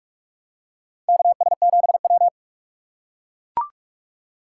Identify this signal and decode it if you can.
Morse code, heard as a keyed tone, about 44 wpm, 700 Hz. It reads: KS7W
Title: USA Call Signs - 4x Character Spacing - 45wpm